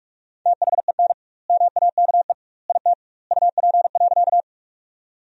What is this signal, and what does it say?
Morse code, heard as a keyed tone, about 46 wpm, 700 Hz. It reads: THEN MAKE IT UP.
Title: 2x Word-Spacing - Sentences from Top 100 Words - 45wpm